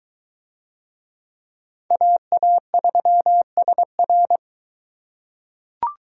Morse code, heard as a keyed tone, about 22 wpm, 700 Hz. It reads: AA3SR